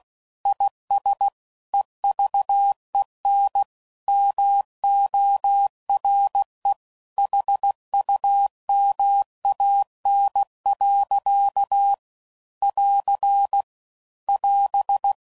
Morse code, heard as a keyed tone, about 16 wpm, 800 Hz. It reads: IS EVEN MORE HUMAN. <AR> <AS>